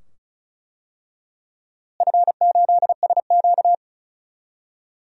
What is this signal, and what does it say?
Morse code, heard as a keyed tone, about 36 wpm, 700 Hz. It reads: F8SQ
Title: DXCC-100 Call Signs - 4 Characters - Repeated 2x - 35wpm